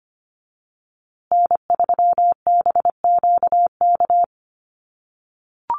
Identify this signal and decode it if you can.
Morse code, heard as a keyed tone, about 25 wpm, 700 Hz. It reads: N3BQK
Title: USA Call Signs Repeated 2x 25wpm